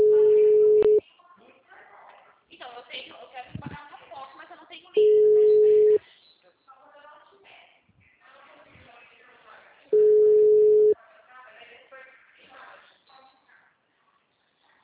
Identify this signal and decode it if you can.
Ringback tone (Europe, 425 Hz), with speech.
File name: out- -2018- - - .5585